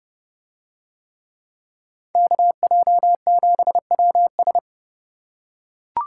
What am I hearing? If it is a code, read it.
Morse code, heard as a keyed tone, about 30 wpm, 700 Hz. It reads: KJ7WS